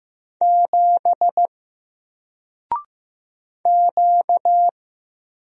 Morse code, heard as a keyed tone, about 15 wpm, 700 Hz. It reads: 7 Q
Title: Single Letter-Number Alt 15wpm